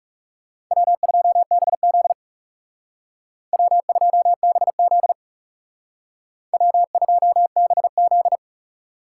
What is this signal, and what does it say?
Morse code, heard as a keyed tone, about 36 wpm, 700 Hz. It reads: W2BZ W2BZ W2BZ
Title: Speed Racing - USA 1x2 Call Signs 30wpm